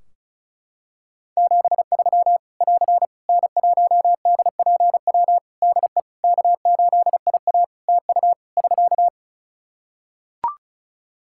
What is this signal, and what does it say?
Morse code, heard as a keyed tone, about 36 wpm, 700 Hz. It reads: 73 <AR> N1DPW DE K8IA TU <SK>
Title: K1USN SST QSO Practice 35wpm